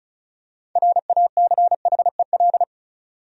Morse code, heard as a keyed tone, about 36 wpm, 700 Hz. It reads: RACHEL
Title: USA Common Names - 35wpm